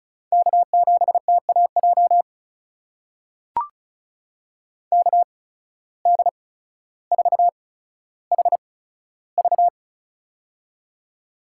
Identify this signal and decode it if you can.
Morse code, heard as a keyed tone, about 36 wpm, spaced as for 15 wpm, 700 Hz. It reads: K7TAJ KD4HV
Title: USA Call Signs - 5 Characters - 8x Character Spacing - 35wpm